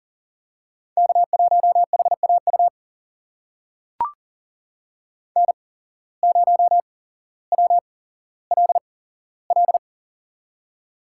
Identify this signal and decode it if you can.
Morse code, heard as a keyed tone, about 40 wpm, spaced as for 17 wpm, 700 Hz. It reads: K1HAU N0WLL
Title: USA Call Signs - 5 Characters - 8x Character Spacing - 40wpm